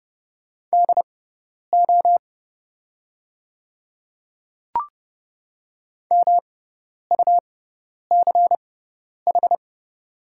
Morse code, heard as a keyed tone, about 30 wpm, spaced as for 10 wpm, 690 Hz. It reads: DO MUCH
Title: 6x Character Spacing - Top 200 Words - 30wpm